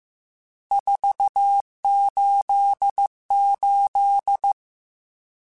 Morse code, heard as a keyed tone, about 15 wpm, 800 Hz. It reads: 488
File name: RSTs / RST-mp3s / Various RST @13